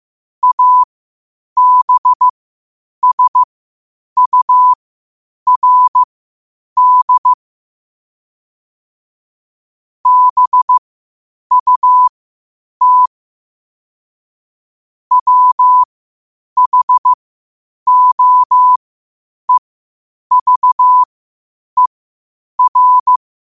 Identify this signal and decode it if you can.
Morse code, heard as a keyed tone, about 15 wpm, spaced as for 8 wpm, 1000 Hz. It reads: ABSURD BUT WHOEVER
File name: qotdfiles / 8wpm